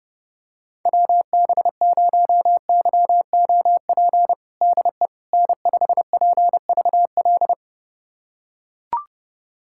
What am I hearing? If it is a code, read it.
Morse code, heard as a keyed tone, about 30 wpm, 700 Hz. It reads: WB0YOP DE N5PVL
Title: USA Callsign DE Callsign 30wpm